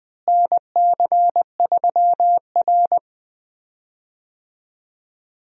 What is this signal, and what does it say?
Morse code, heard as a keyed tone, about 20 wpm, 700 Hz. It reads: NC3R
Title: USA Call Signs - 4 Characters - Repeated 2x - 20wpm